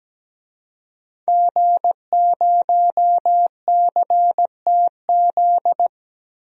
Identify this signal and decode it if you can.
Morse code, heard as a keyed tone, about 17 wpm, 700 Hz. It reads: G0CTZ
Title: DXCC-100 Call Signs - 5 Characters - 17wpm